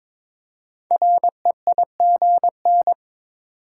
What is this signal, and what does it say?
Morse code, heard as a keyed tone, about 22 wpm, 700 Hz. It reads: REIGN
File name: 5 Letter Words - First 3 Letters Given - 22wpm